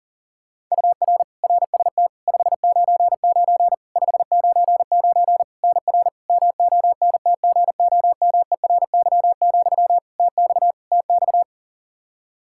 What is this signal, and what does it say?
Morse code, heard as a keyed tone, about 40 wpm, 700 Hz. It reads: UR RST 599 599 NR MONTGOMERY, TX TX
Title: Practice QSO - Exchange 1 - Beginning - 40wpm